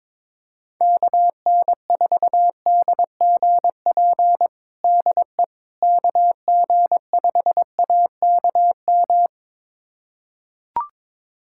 Morse code, heard as a keyed tone, about 22 wpm, 700 Hz. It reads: KN4DGP DE KG5AKM
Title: USA Callsign DE Callsign - 6 Characters - 22wpm